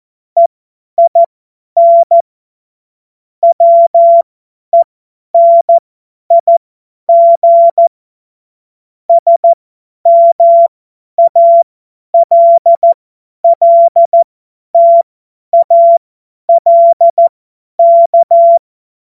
Morse code, heard as a keyed tone, about 14 wpm, spaced as for 10 wpm, 680 Hz. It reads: EIN WENIG SMALLTALK
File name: Morsekurs / Witze / Witz 11. Mai 2021, 50 F70